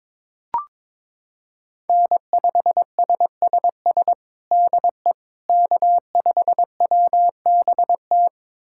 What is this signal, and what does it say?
Morse code, heard as a keyed tone, about 21 wpm, 700 Hz. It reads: N5SSS DE K5WBT
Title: USA Callsign DE Callsign - 5 Characters - 22wpm